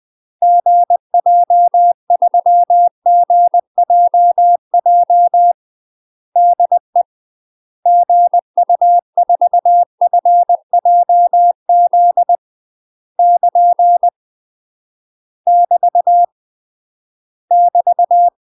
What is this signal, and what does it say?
Morse code, heard as a keyed tone, about 20 wpm, 700 Hz. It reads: GJ3GJJ DE GU4FJZ <KN> <BT> <BT>
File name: UK MASTER QSOs TRAINING / 20WPM / UK MASTER QSOs 0026